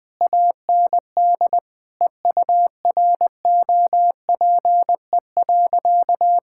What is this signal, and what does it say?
Morse code, heard as a keyed tone, about 20 wpm, 700 Hz. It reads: AND EUROPE.
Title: Sentences with 5-Word Phrases - 20wpm